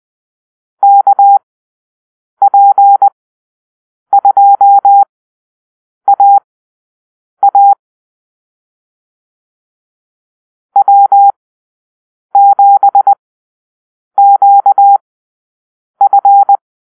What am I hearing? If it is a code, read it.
Morse code, heard as a keyed tone, about 20 wpm, spaced as for 7 wpm, 800 Hz. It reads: KP2AA W7QF
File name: arrl / Call-Sign Drill-7